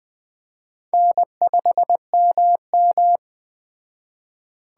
Morse code, heard as a keyed tone, about 20 wpm, 700 Hz. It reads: N5MM